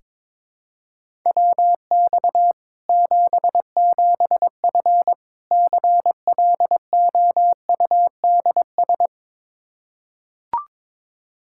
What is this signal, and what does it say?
Morse code, heard as a keyed tone, about 22 wpm, 700 Hz. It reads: WX 77F CLOUDS
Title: Practice QSO - Exchange 2 - Info - 22wpm